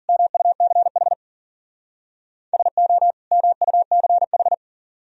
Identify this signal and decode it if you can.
Morse code, heard as a keyed tone, about 48 wpm, 700 Hz. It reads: MUCH SO MUCH
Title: Speed-Racing - Sets of 2 Words 40wpm